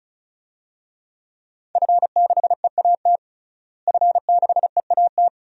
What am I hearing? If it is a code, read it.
Morse code, heard as a keyed tone, about 34 wpm, 700 Hz. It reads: F6EAT F6EAT
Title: DXCC-100 Call Signs - 5 Characters - Repeated 3x - 35wpm